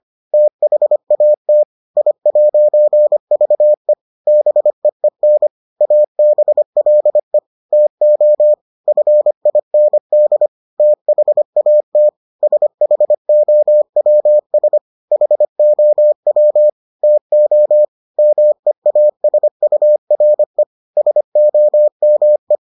Morse code, heard as a keyed tone, about 25 wpm, 600 Hz. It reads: THAT I'VE BEEN ABLE TO FIND THAT SHOWS HOW TO MEASURE SOME